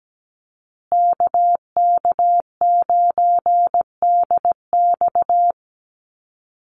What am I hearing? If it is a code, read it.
Morse code, heard as a keyed tone, about 17 wpm, 700 Hz. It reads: KK9DX
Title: Call Signs Repeated 2x - 17wpm